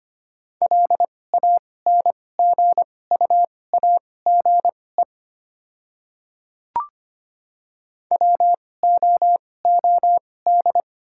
Morse code, heard as a keyed tone, about 25 wpm, spaced as for 18 wpm, 690 Hz. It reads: LANGUAGE WOOD
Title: 2x Character Spacing - Top 500 Words - 25wpm